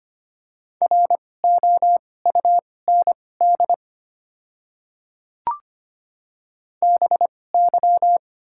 Morse code, heard as a keyed tone, about 25 wpm, spaced as for 18 wpm, 690 Hz. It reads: ROUND BY
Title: 2x Character Spacing - Top 200 Words - 25wpm